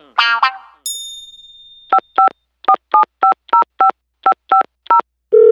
DTMF, heard as telephone touch tones, tones of about 110 ms. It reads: 5547505550